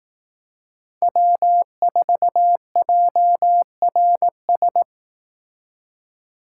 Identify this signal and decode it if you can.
Morse code, heard as a keyed tone, about 19 wpm, 700 Hz. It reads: W4JRS
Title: Speed Racing - USA 1x3 Call Signs 15wpm